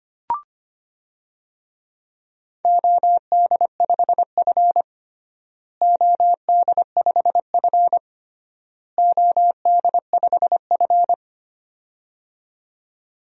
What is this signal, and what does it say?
Morse code, heard as a keyed tone, about 25 wpm, 700 Hz. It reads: OD5F OD5F OD5F
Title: DXCC-100 Call Signs - 4 Characters - Repeated 3x - 25wpm